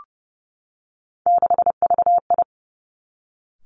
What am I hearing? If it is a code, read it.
Morse code, heard as a keyed tone, about 32 wpm, 700 Hz. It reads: 6VI
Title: Three Challenging Letter-Number Combinations 30wpm